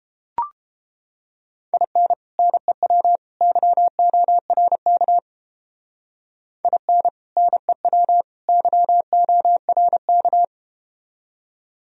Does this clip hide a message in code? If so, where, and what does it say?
Yes: Morse code from 1.7 to 10.4 s, heard as a keyed tone, about 32 wpm, 700 Hz. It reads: IN NEW YORK IN NEW YORK